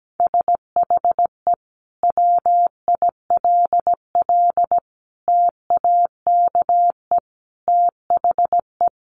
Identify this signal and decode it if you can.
Morse code, heard as a keyed tone, about 17 wpm, 700 Hz. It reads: SHE WILL TAKE THE
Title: Sentences from Top 1500 Words 17wpm